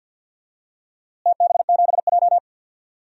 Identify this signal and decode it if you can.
Morse code, heard as a keyed tone, about 50 wpm, 700 Hz. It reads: TB7J
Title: DXCC-100 Call Signs - 4 Characters - Repeated 2x - 50wpm